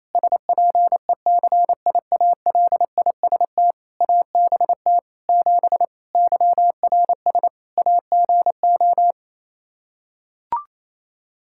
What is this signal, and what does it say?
Morse code, heard as a keyed tone, about 28 wpm, 700 Hz. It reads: SPECIALIST ABT 7 YRS AGO
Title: Practice QSO - Exchange 3 - More Info - 28wpm